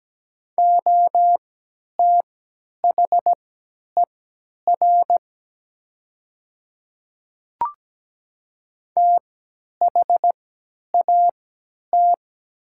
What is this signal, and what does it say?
Morse code, heard as a keyed tone, about 17 wpm, spaced as for 10 wpm, 700 Hz. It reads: OTHER THAT
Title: 3x Character Spacing - Top 100 Words - 17wpm